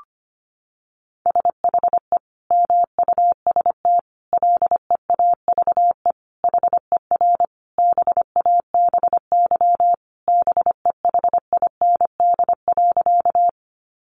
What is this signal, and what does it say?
Morse code, heard as a keyed tone, about 25 wpm, 700 Hz. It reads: SHE MUST LEAVE HER BABY BEHIND.
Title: Sentences from Top 1300 Words 25wpm